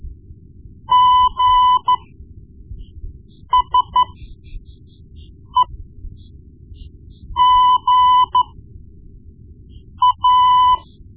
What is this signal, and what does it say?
Morse code, heard as a keyed tone, about 13 wpm, spaced as for 5 wpm, 1000 Hz. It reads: GSEGA
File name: Morse Lessons / Morse Lessons / Tape-1B